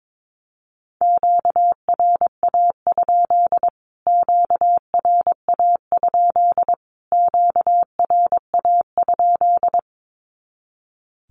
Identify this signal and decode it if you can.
Morse code, heard as a keyed tone, about 22 wpm, 700 Hz. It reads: QRA? QRA? QRA?
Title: Q-Codes Repeated 3x 22wpm